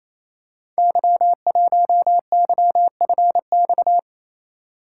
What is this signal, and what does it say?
Morse code, heard as a keyed tone, about 29 wpm, 700 Hz. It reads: Y1YFX
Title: Code Groups - 5 Letter:Numbers - 28wpm